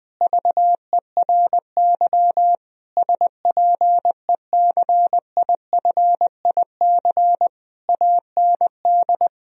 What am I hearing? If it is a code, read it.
Morse code, heard as a keyed tone, about 20 wpm, 700 Hz. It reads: VERY SPECIFIC AND
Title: Sentences from Top 1000 Words - Review Part 3 - 20wpm